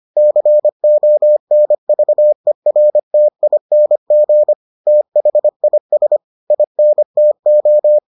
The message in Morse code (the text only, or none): CONVERTING THIS INTO